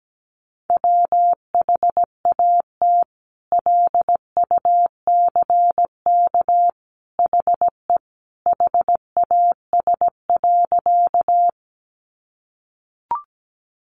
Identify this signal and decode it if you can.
Morse code, heard as a keyed tone, about 17 wpm, 700 Hz. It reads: WHAT LUCK HE HAS.
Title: Sentences from Top 1500 Words 17wpm